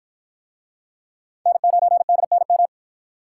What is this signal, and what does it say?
Morse code, heard as a keyed tone, about 53 wpm, 700 Hz. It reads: N9DNM